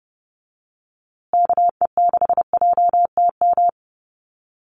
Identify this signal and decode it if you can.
Morse code, heard as a keyed tone, about 30 wpm, 700 Hz. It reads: KE6JTM
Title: USA Call Signs Repeated 3x 30wpm